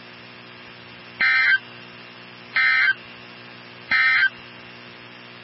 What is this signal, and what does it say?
Emergency Alert System (SAME) tones, heard as harsh buzzing bursts. It is heard from 1.1 to 4.4 s.
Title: NOAA Weather Radio 162.550